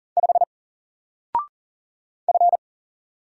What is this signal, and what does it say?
Morse code, heard as a keyed tone, about 39 wpm, 700 Hz. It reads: 5 F
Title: Single Character (Letter:Number:Punctuation:Prosign) 40wpm